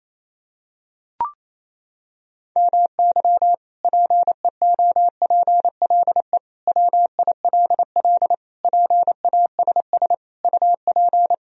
Morse code, heard as a keyed tone, about 28 wpm, 700 Hz. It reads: MY PEOPLE WILL PASS UP
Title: Sentences from Top 1000 Words - Review Part 2 - 28wpm